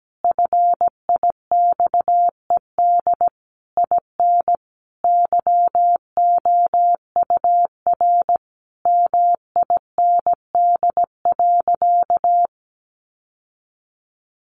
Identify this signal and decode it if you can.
Morse code, heard as a keyed tone, about 17 wpm, 700 Hz. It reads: FIXED IN YOUR MIND.
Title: Sentences from Top 1100 Words 17wpm